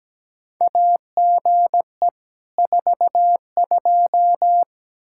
Morse code, heard as a keyed tone, about 17 wpm, 700 Hz. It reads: AGE 42
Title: Practice QSO - Exchange 3 - More Info - 17wpm